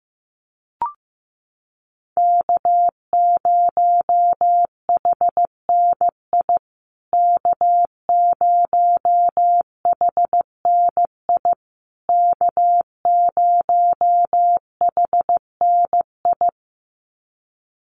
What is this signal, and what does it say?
Morse code, heard as a keyed tone, about 15 wpm, 700 Hz. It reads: K0HNI K0HNI K0HNI
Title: USA Call Signs Repeated 3x 15wpm